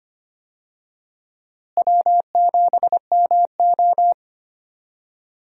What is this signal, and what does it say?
Morse code, heard as a keyed tone, about 25 wpm, 700 Hz. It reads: W7MO
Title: USA Call Signs - 4 Characters - Repeated 3x - 25wpm